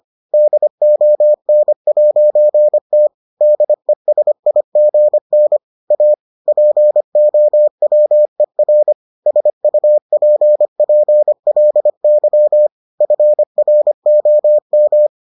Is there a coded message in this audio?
Yes: Morse code from 0.3 to 15.1 s, heard as a keyed tone, about 25 wpm, 600 Hz. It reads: DON'T DESIGN A POWER SUPPLY FROM